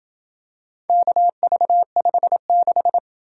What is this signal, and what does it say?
Morse code, heard as a keyed tone, about 27 wpm, 700 Hz. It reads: KV56